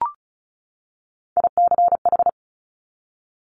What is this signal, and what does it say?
Morse code, heard as a keyed tone, about 36 wpm, 700 Hz. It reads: ICH
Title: Top Three Letter Combinations in Words 35wpm